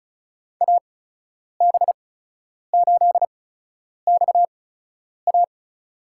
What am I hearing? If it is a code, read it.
Morse code, heard as a keyed tone, about 36 wpm, spaced as for 10 wpm, 700 Hz. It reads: AB8XA
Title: USA Call Signs - 8x Character Spacing - 35wpm